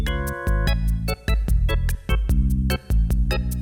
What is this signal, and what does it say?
There is no code.